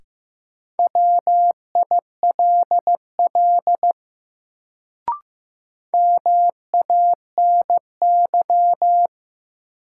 Morse code, heard as a keyed tone, about 15 wpm, 700 Hz. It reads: WILL MANY
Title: Sentences from Top 100 Words - New Word Drill - 15wpm